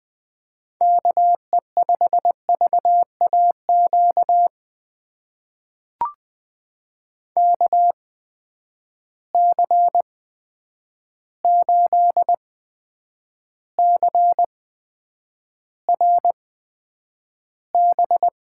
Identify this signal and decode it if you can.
Morse code, heard as a keyed tone, about 20 wpm, spaced as for 9 wpm, 700 Hz. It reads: KE5VAQ KC8CRB
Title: USA Call Signs - 8x Character Spacing - 20wpm